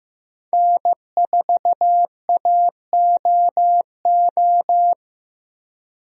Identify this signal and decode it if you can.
Morse code, heard as a keyed tone, about 15 wpm, 700 Hz. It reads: N4AOO
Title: USA Call Signs - 5 Characters - Repeated 2x - 15wpm